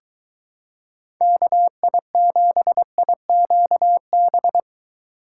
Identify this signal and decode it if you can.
Morse code, heard as a keyed tone, about 23 wpm, 700 Hz. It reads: KI7IQB